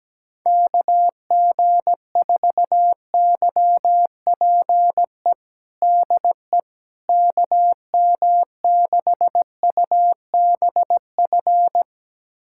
Morse code, heard as a keyed tone, about 17 wpm, 700 Hz. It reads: KG4YPE DE KM6UBF
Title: USA Callsign DE Callsign 17wpm